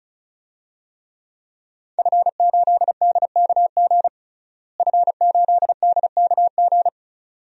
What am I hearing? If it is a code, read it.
Morse code, heard as a keyed tone, about 34 wpm, 700 Hz. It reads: F8DKG F8DKG
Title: DXCC-100 Call Signs - 5 Characters - Repeated 2x - 35wpm